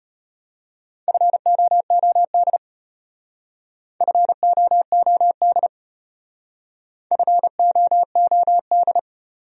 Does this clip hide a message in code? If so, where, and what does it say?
Yes: Morse code from 1.1 to 9.0 s, heard as a keyed tone, about 33 wpm, 700 Hz. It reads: FOOD FOOD FOOD